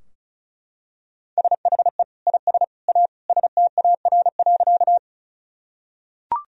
Morse code, heard as a keyed tone, about 34 wpm, 700 Hz. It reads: SHE IS A STAR.
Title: Sentences from Top 1000 Words - Review Part 2 - 35wpm